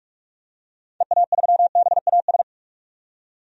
Morse code, heard as a keyed tone, about 44 wpm, 700 Hz. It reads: EA3BAS
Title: DXCC-100 Call Signs - 6 Characters - 45wpm